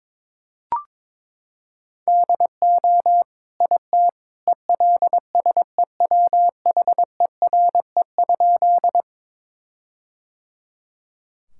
Morse code, heard as a keyed tone, about 22 wpm, 700 Hz. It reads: DO IT ELSEWHERE?
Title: Sentences from Top 1600 Words 22wpm